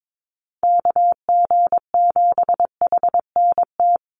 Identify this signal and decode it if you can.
Morse code, heard as a keyed tone, about 22 wpm, 700 Hz. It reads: KG7HNT
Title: USA Call Signs 22wpm